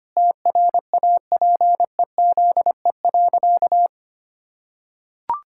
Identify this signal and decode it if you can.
Morse code, heard as a keyed tone, about 25 wpm, 700 Hz. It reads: TRAPEZE.
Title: Pangram Challenge v2 - 25wpm